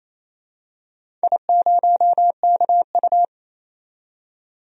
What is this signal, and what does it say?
Morse code, heard as a keyed tone, about 28 wpm, 700 Hz. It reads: I0KU
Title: DXCC-100 Call Signs - 4 Characters - 28wpm